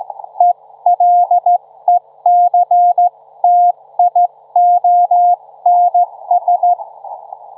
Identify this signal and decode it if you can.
Morse code, heard as a keyed tone, about 15 wpm, 710 Hz. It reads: ELECTIONS
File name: Trivia-ZL2WT